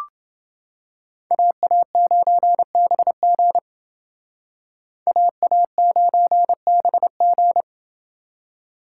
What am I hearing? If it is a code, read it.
Morse code, heard as a keyed tone, about 28 wpm, 700 Hz. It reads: AA9BG AA9BG